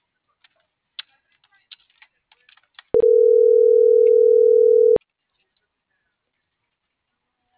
Ringback tone (North America), 440 + 480 Hz.